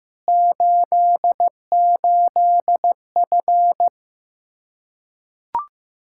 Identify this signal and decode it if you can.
Morse code, heard as a keyed tone, about 15 wpm, 700 Hz. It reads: 88F
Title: Practice QSO - Exchange 2 - Info - 15wpm